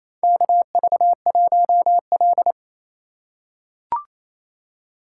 Morse code, heard as a keyed tone, about 28 wpm, 700 Hz. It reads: KV1L
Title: USA Call Signs - 3x Character Spacing - 28wpm